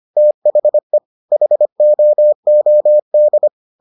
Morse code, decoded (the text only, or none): THE HOOD